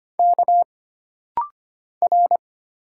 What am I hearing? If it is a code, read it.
Morse code, heard as a keyed tone, about 25 wpm, 700 Hz. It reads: K R
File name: Single Character (Letter:Number:Punctuation:Prosign) - Rapid-Fire - 25wpm